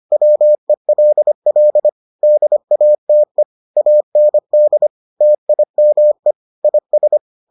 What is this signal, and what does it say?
Morse code, heard as a keyed tone, about 25 wpm, 600 Hz. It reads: WELL DATE AND TIME IS